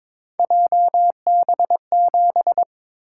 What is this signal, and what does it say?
Morse code, heard as a keyed tone, about 22 wpm, 700 Hz. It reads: JB7
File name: practice / 22wpm / Three Letter-Number Combinations - 22wpm